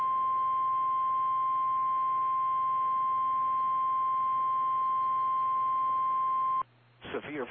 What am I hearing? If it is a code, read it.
Emergency Alert System (SAME) tones (the 1050 Hz attention tone), with speech. It is heard from 0.4 to 6.6 s.
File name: eas / KKPRKXPN / KKPRKXPN